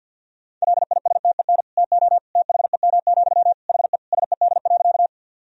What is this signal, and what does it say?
Morse code, heard as a keyed tone, about 52 wpm, 700 Hz. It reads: LISTEN TO THEM, HE SED.
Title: Sentences from Top 400 Words and Abbreviations 50wpm